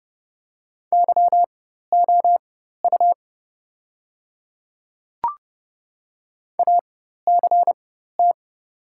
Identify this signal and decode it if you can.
Morse code, heard as a keyed tone, about 31 wpm, spaced as for 14 wpm, 700 Hz. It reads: YOU ACT
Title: 4x Character Spacing - Top 200 Words - 30wpm